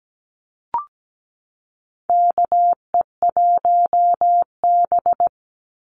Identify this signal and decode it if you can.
Morse code, heard as a keyed tone, about 17 wpm, 700 Hz. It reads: KE1B
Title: Call Signs Repeated 2x - 17wpm